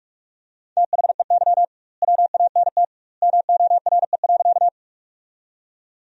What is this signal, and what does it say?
Morse code, heard as a keyed tone, about 46 wpm, 700 Hz. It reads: THEY WANT MORE.
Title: 2x Word-Spacing - Sentences from Top 200 Words - 45wpm